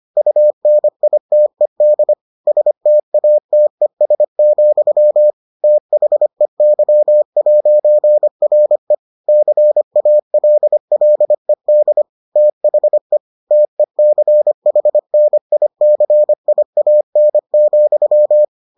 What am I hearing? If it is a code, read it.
Morse code, heard as a keyed tone, about 25 wpm, 600 Hz. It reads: UNITED STATES, THEY'RE CALLED THE TECHNICIAN,